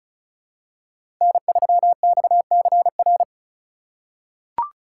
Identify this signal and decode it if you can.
Morse code, heard as a keyed tone, about 36 wpm, 700 Hz. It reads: N3XCR
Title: USA Call Signs - 3x Character Spacing - 35wpm